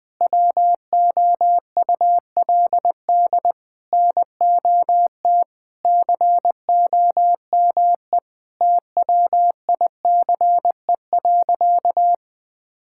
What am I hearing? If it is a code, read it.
Morse code, heard as a keyed tone, about 20 wpm, 700 Hz. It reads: WOULD NOT COME TWICE.
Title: Sentences from Top 1600 Words 20wpm